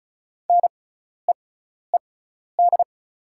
Morse code, heard as a keyed tone, about 36 wpm, spaced as for 12 wpm, 690 Hz. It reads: NEED
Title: 6x Character Spacing - Top 300 Words - 35wpm